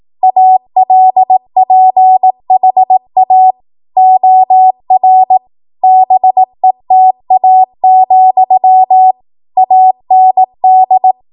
Morse code, heard as a keyed tone, about 18 wpm, 750 Hz. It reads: ALPHA OR BETA, AND
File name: Morse / 18WPM